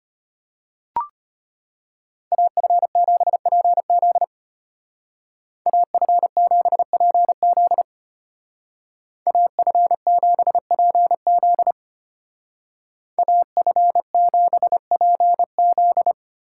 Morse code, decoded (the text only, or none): AF7PZ AF7PZ AF7PZ AF7PZ